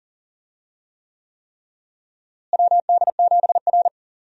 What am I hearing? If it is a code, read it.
Morse code, heard as a keyed tone, about 39 wpm, 700 Hz. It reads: WD7R
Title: USA Call Signs - 4 Characters - Repeated 3x - 40wpm